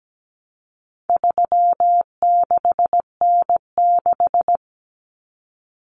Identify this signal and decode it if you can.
Morse code, heard as a keyed tone, about 17 wpm, 700 Hz. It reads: 36N6
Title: Common Cut Numbers - 4-Digit 17wpm